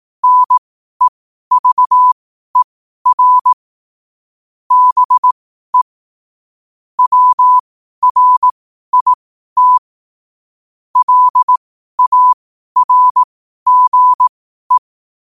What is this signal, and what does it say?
Morse code, heard as a keyed tone, about 18 wpm, spaced as for 12 wpm, 1000 Hz. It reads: NEVER BE WRIT LARGE
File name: qotdfiles / 12wpm